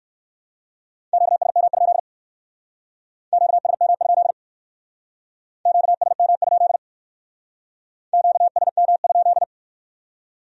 Morse code, heard as a keyed tone, about 53 wpm, 700 Hz. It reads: QSM? QSM? QSM? QSM?
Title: Speed-Racing Q-Codes - 45wpm